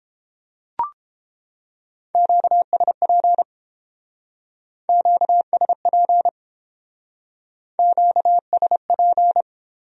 Morse code, heard as a keyed tone, about 28 wpm, 700 Hz. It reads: QSP QSP QSP